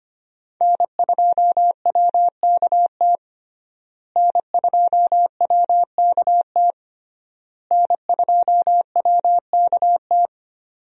Morse code, heard as a keyed tone, about 25 wpm, 700 Hz. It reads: N2WKT N2WKT N2WKT
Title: USA Call Signs - 5 Characters - Repeated 3x - 25wpm